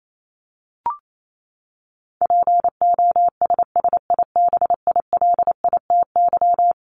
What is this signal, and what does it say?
Morse code, heard as a keyed tone, about 28 wpm, 700 Hz. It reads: POSSIBILITY
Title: Top 1751-2000 Words 28wpm